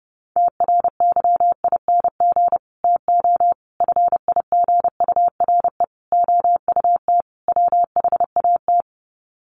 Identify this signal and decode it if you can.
Morse code, heard as a keyed tone, about 30 wpm, 700 Hz. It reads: TRYING TO FIGURE OUT WHAT
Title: Sets of 5 Words 30wpm